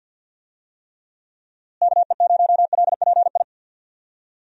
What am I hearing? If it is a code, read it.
Morse code, heard as a keyed tone, about 52 wpm, 700 Hz. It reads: KE0LPI
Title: USA Call Signs - 6 Characters - Repeated 3x - 50wpm